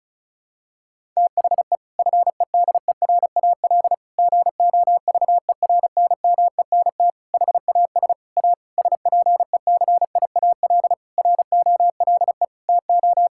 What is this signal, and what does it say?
Morse code, heard as a keyed tone, about 36 wpm, 700 Hz. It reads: THE FEDERAL GOVERNMENT HAS A SPECIAL ROLE TO
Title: Sentences from Top 1000 Words - Review Part 6 - 35wpm